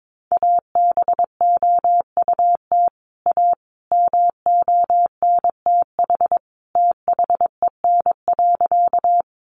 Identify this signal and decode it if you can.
Morse code, heard as a keyed tone, about 22 wpm, 700 Hz. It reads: ABOUT A MONTH THEN.
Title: Sentences from Top 800 Words 22wpm